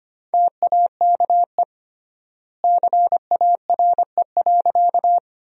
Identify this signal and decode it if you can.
Morse code, heard as a keyed tone, about 25 wpm, 700 Hz. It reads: TAKE CARE.
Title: 3x Word-Spacing - Sentences from Top 300 Words - 25wpm